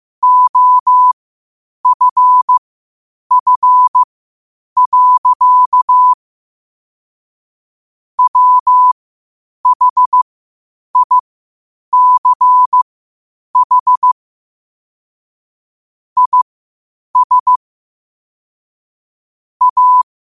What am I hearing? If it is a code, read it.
Morse code, heard as a keyed tone, about 15 wpm, spaced as for 8 wpm, 1000 Hz. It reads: OFF. WHICH IS A
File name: qotdfiles / 8wpm